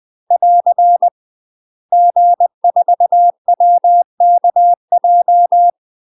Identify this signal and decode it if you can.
Morse code, heard as a keyed tone, about 20 wpm, 700 Hz. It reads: <AR> G4WKJ
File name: UK MASTER QSOs TRAINING / 20WPM / UK MASTER QSOs 0036